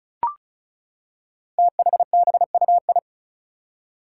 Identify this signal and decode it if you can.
Morse code, heard as a keyed tone, about 36 wpm, 700 Hz. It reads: THBUI
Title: Five Challenging Letter-Number Combinations 35wpm